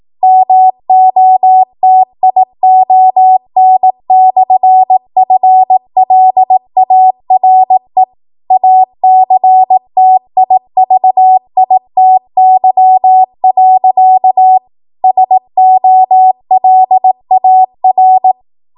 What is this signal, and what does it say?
Morse code, heard as a keyed tone, about 18 wpm, 750 Hz. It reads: MOTION/FLARE ACTIVITY. SOLAR
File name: Morse / 18WPM